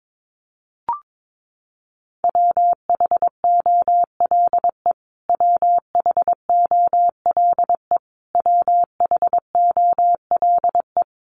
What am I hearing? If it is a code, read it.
Morse code, heard as a keyed tone, about 22 wpm, 700 Hz. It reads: WHOLE WHOLE WHOLE
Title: Top 500 Words Repeated 22wpm